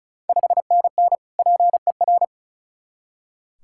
Morse code, heard as a keyed tone, about 36 wpm, 690 Hz. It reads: 5NN PER
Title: Ontario QSO Party Exchange 35wpm